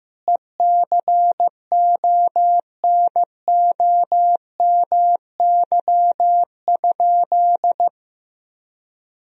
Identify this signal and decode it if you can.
Morse code, heard as a keyed tone, about 15 wpm, 700 Hz. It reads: ECONOMY?
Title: Sentences from Top 1000 Words - Review Part 3 - 15wpm